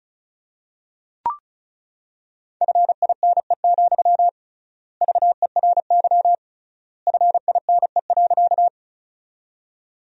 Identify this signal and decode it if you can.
Morse code, heard as a keyed tone, about 36 wpm, 700 Hz. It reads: FINE, VERY FINE.
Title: 3x Word-Spacing - Sentences from Top 500 Words - 35wpm